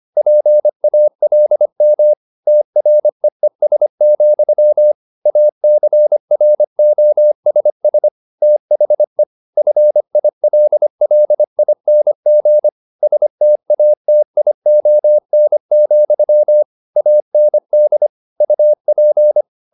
Morse code, heard as a keyed tone, about 25 wpm, 600 Hz. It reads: PALM TREES, ACROSS THE FILLING STATION, AND UP